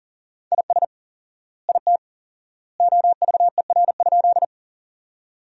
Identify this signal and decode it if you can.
Morse code, heard as a keyed tone, about 40 wpm, 700 Hz. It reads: IS IT OVER?
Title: 4x Word-Spacing - Sentences from Top 100 Words - 40wpm